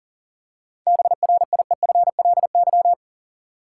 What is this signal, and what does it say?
Morse code, heard as a keyed tone, about 40 wpm, 700 Hz. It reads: BRIEFLY
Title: Seven Letter Words - 40wpm